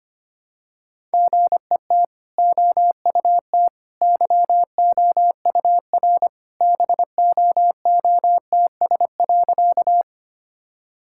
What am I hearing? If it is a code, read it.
Morse code, heard as a keyed tone, about 25 wpm, 690 Hz. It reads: GET OUT YOUR BOOTS.
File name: Easy Sentences with Hints - 25wpm